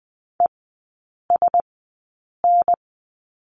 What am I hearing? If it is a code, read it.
Morse code, heard as a keyed tone, about 20 wpm, spaced as for 8 wpm, 700 Hz. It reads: ESN
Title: 035 Lesson 7e - Letter s - Three Characters 20wpm